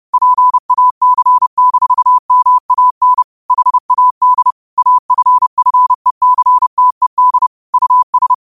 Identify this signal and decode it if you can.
Morse code, heard as a keyed tone, about 30 wpm, 1000 Hz. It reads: PAC-MAN HAD AFFECTED US
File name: qotdfiles / 30wpm